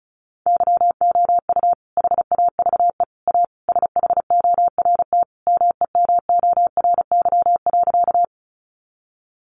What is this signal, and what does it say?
Morse code, heard as a keyed tone, about 34 wpm, 700 Hz. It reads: YOU HAVE A SHORT MEMORY.
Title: Sentences from Top 1300 Words 35wpm